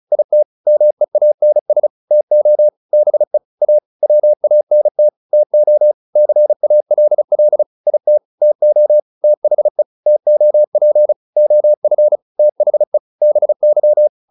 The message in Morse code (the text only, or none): IT MEANS TO BE A WANT TO CALL IT TO THE TOP OF THE BY